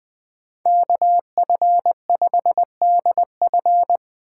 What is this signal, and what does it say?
Morse code, heard as a keyed tone, about 20 wpm, 700 Hz. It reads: KF5DF
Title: Speed Racing - USA 2x2 Call Signs 17wpm